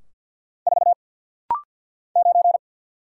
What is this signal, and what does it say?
Morse code, heard as a keyed tone, about 49 wpm, 700 Hz. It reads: 4 9
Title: Single Letter-Number - Rapid-Fire 50wpm